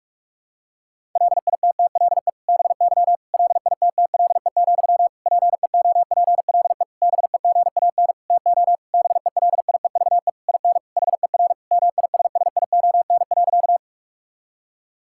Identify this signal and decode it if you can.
Morse code, heard as a keyed tone, about 46 wpm, 700 Hz. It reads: LITTLE BY LITTLE, PEOPLE BEGAN TO BELIEVE IN HER MISSION.
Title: Sentences from Top 1600 Words 45wpm